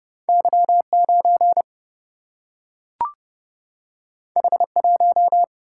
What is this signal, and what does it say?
Morse code, heard as a keyed tone, about 30 wpm, 700 Hz. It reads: Y9 H1
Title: Single-Letter Single-Number 30wpm